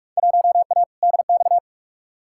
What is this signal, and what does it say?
Morse code, heard as a keyed tone, about 46 wpm, 700 Hz. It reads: 1A DX
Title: Field Day Exchanges Encore - 45wpm